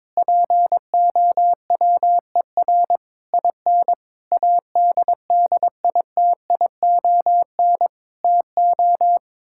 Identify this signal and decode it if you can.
Morse code, heard as a keyed tone, about 22 wpm, 700 Hz. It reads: POWER IN ADDITION TO
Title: Sentences from Top 1000 Words - Review Part 6 - 22wpm